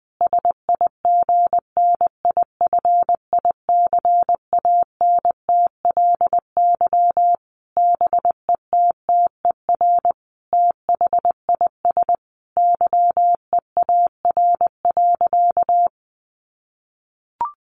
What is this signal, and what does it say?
Morse code, heard as a keyed tone, about 20 wpm, 700 Hz. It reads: SIGNIFICANTLY BETTER THIS YEAR.
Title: Sentences from Top 1500 Words 20wpm